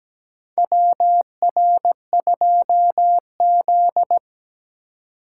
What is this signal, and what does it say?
Morse code, heard as a keyed tone, about 17 wpm, 700 Hz. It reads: WR2Z
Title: USA Call Signs - 4 Characters - 3x Character Spacing - 17wpm